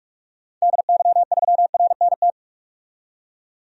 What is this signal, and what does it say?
Morse code, heard as a keyed tone, about 46 wpm, 700 Hz. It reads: DY3RNT
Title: DXCC-100 Call Signs - 6 Characters - 45wpm